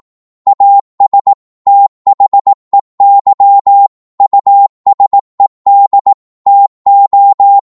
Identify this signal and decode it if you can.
Morse code, heard as a keyed tone, about 18 wpm, 800 Hz. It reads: AS THEY USED TO